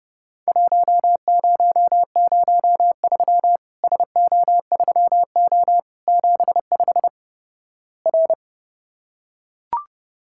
Morse code, heard as a keyed tone, about 31 wpm, 700 Hz. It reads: 1003 SO3O 75 R
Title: Worked All Europe DX Contest - QTC Exchange - 30wpm